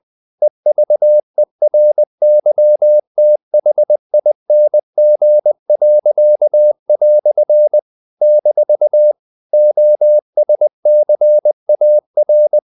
Morse code, heard as a keyed tone, about 20 wpm, 600 Hz. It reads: EVERYTHING." - OSCAR
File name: CW / Practice / EnglishQuotations0320